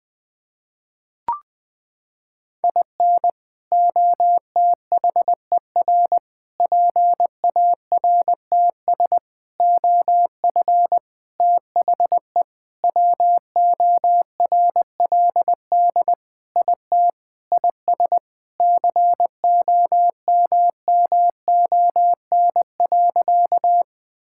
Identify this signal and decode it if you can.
Morse code, heard as a keyed tone, about 20 wpm, 700 Hz. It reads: IN OTHER PARTS OF THE WORLD IT IS COMMON.
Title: Sentences with 6-Word Phrases - 20wpm